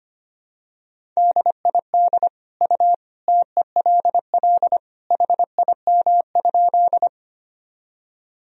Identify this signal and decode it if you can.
Morse code, heard as a keyed tone, about 25 wpm, 700 Hz. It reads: DID U TELL HIM?
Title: Sentences from Top 200 Words and Abbreviations 25wpm